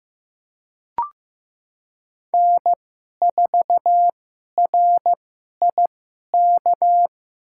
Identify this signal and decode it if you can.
Morse code, heard as a keyed tone, about 15 wpm, spaced as for 11 wpm, 700 Hz. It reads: N4RIK
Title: USA Call Signs - 5 Characters - 2x Character Spacing - 15wpm